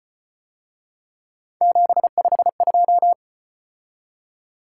Morse code, heard as a keyed tone, about 34 wpm, 700 Hz. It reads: 752